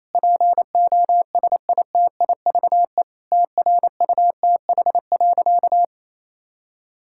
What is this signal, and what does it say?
Morse code, heard as a keyed tone, about 28 wpm, 700 Hz. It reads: POSITIVE TRUTH.
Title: Sentences from Top 1000 Words - Review Part 5 - 28wpm